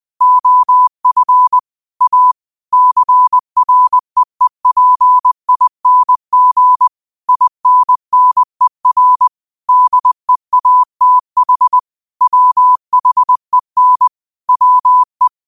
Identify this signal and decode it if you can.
Morse code, heard as a keyed tone, about 20 wpm, 1000 Hz. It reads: OF A CREEPING INNER DEATH WHEN WE
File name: qotdfiles / 20wpm